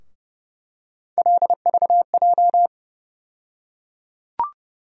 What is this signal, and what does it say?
Morse code, heard as a keyed tone, about 31 wpm, 700 Hz. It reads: LVJ
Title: USA Call Sign Postfixes - 3 Letters 30wpm